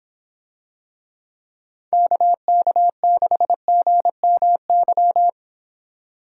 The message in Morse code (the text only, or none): KK6GMY